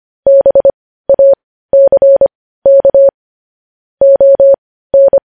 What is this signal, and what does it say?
Morse code, heard as a keyed tone, about 25 wpm, spaced as for 15 wpm, 550 Hz. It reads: BACK ON
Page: news headlines in morse code at 15 wpm